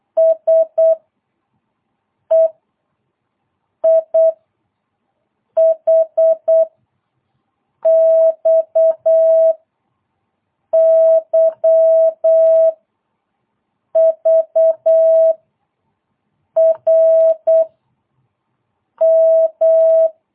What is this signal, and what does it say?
Morse code, heard as a keyed tone, about 8 wpm, 650 Hz. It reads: S E I H X Y V R M
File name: jh8cbh / 2022cw